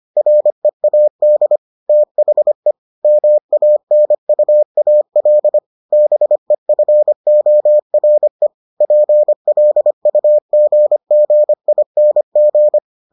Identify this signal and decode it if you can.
Morse code, heard as a keyed tone, about 25 wpm, 600 Hz. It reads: READ THE MANUAL BEFORE PLUGGING